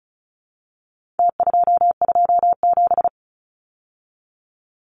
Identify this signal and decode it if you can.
Morse code, heard as a keyed tone, about 34 wpm, 700 Hz. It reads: T227
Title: Numbers - Contest Style - 4-Digit 35wpm